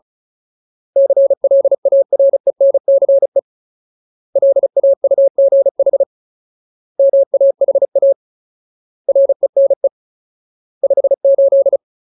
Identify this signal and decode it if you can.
Morse code, heard as a keyed tone, about 34 wpm, 550 Hz. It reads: CLARENCE LAUGH MAHA RENE 58
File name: Ultimate Word List / 35WPM Ultimate